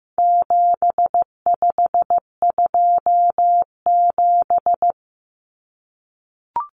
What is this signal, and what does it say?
Morse code, heard as a keyed tone, about 15 wpm, 700 Hz. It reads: 7527
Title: Numbers - 4-Digit 15wpm